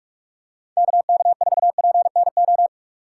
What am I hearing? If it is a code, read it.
Morse code, heard as a keyed tone, about 46 wpm, 700 Hz. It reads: KK4PNO
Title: USA Call Signs - 6 Characters - 2x Character Spacing - 45wpm